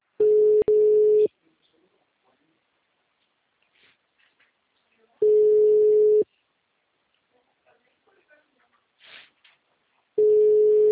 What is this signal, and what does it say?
Ringback tone (Europe, 425 Hz), with speech.